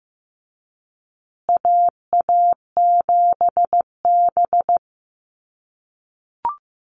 Morse code, heard as a keyed tone, about 15 wpm, 700 Hz. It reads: AA7B
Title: USA Call Signs Repeated 2x 15wpm